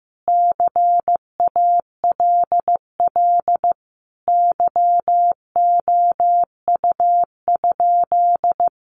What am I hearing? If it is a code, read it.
Morse code, heard as a keyed tone, about 15 wpm, 700 Hz. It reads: CALL YOU?
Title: Sentences from Top 100 Words 15wpm